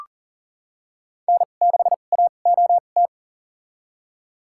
Morse code, heard as a keyed tone, about 38 wpm, spaced as for 28 wpm, 700 Hz. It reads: N6AOT
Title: USA Call Signs - 2x Character Spacing - 40wpm